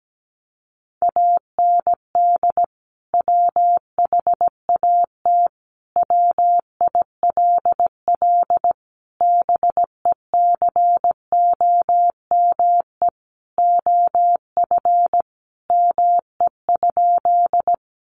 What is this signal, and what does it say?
Morse code, heard as a keyed tone, about 17 wpm, 700 Hz. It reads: AND WHAT WILL BECOME OF ME?
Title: Sentences from Top 600 Words 17wpm